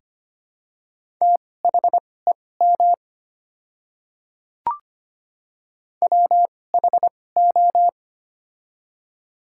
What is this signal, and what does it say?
Morse code, heard as a keyed tone, about 25 wpm, spaced as for 18 wpm, 700 Hz. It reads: THEM WHO
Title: 2x Character Spacing - Top 100 Words - 25wpm